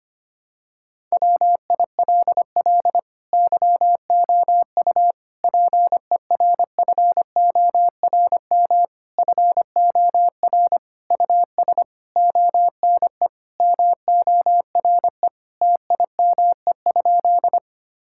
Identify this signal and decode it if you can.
Morse code, heard as a keyed tone, about 25 wpm, 700 Hz. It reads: WILL YOU PERFORM FOR US ONE MORE TIME?
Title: Sentences from Top 1000 Words - Review Part 5 - 25wpm